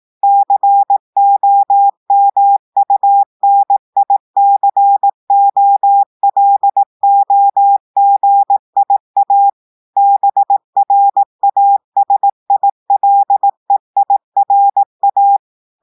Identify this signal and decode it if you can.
Morse code, heard as a keyed tone, about 18 wpm, 800 Hz. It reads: COMUNICOLOGIA BRASILEIRA
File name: audios / portugais / portugais texte en clair 1 vitesse 18 mots minutes